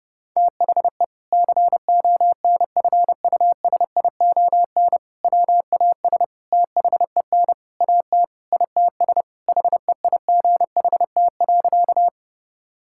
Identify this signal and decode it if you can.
Morse code, heard as a keyed tone, about 30 wpm, 700 Hz. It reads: THE CONFUSION WAS THEN AT ITS HEIGHT.
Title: Sentences from Top 1600 Words 30wpm